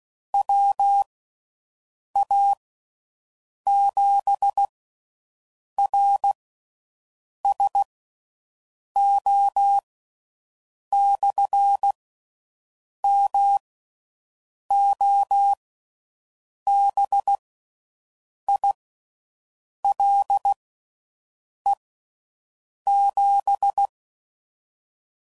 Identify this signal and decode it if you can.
Morse code, heard as a keyed tone, about 16 wpm, spaced as for 6 wpm, 800 Hz. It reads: WA7RSO/MOBILE7